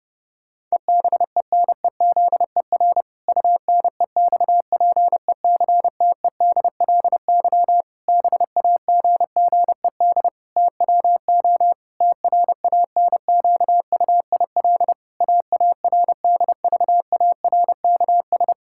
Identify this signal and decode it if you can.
Morse code, heard as a keyed tone, about 30 wpm, 700 Hz. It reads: EBENEZER UNEXPECTEDLY BAGGED TWO TRANQUIL AARDVARKS
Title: Pangram Challenge v2 - 30wpm